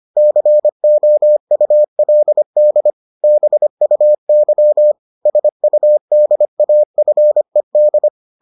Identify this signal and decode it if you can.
Morse code, heard as a keyed tone, about 25 wpm, 600 Hz. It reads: COULD BUY SUDAFED